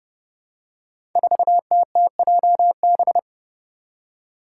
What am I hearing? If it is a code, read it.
Morse code, heard as a keyed tone, about 30 wpm, 700 Hz. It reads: 4TTJB
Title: Five Challenging Letter-Number Combinations 30wpm